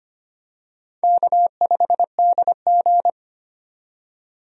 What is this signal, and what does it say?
Morse code, heard as a keyed tone, about 25 wpm, 700 Hz. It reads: K5DG
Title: USA Call Signs - 4 Characters - 25wpm